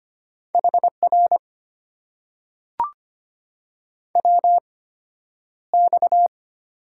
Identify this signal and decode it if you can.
Morse code, heard as a keyed tone, about 25 wpm, spaced as for 11 wpm, 690 Hz. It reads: HR WX
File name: Top 90 QSO Elements - 8x Character Spacing - 25wpm